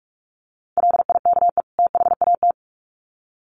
Morse code, heard as a keyed tone, about 44 wpm, 700 Hz. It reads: LIKE THAT
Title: Sets of 2 Words 45wpm